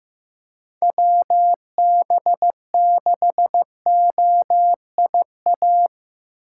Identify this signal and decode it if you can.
Morse code, heard as a keyed tone, about 15 wpm, 700 Hz. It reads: WB6OIA
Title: USA Call Signs - 6 Characters - 8x Character Spacing - 15wpm